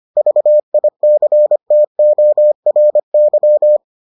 Morse code, heard as a keyed tone, about 25 wpm, 600 Hz. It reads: VICTORY